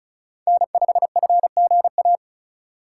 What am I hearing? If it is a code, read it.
Morse code, heard as a keyed tone, about 36 wpm, 700 Hz. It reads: N5FGA
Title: USA Call Signs - 5 Characters - 3x Character Spacing - 35wpm